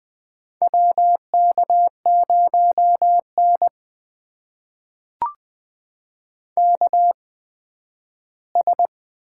Morse code, heard as a keyed tone, about 20 wpm, spaced as for 12 wpm, 700 Hz. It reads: WK0N KS
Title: USA Call Signs - 4 Characters - 8x Character Spacing - 20wpm